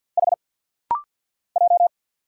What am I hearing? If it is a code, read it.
Morse code, heard as a keyed tone, about 50 wpm, 700 Hz. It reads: H J
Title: Single Letter-Number - Rapid-Fire 50wpm